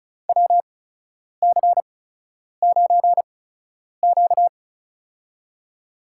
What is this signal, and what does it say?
Morse code, heard as a keyed tone, about 36 wpm, spaced as for 10 wpm, 700 Hz. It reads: WC9Q
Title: USA Call Signs - 8x Character Spacing - 35wpm